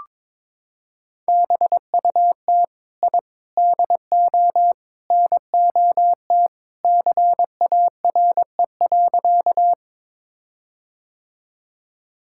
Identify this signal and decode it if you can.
Morse code, heard as a keyed tone, about 22 wpm, 700 Hz. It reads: BUT I DO NOT CARE.
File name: practice / 22wpm / Sentences with 4-Word Phrases - 22wpm